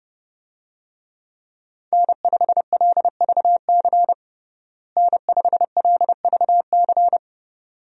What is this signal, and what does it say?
Morse code, heard as a keyed tone, about 31 wpm, 700 Hz. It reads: N5LVC N5LVC
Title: USA Call Signs - 5 Characters - Repeated 2x - 30wpm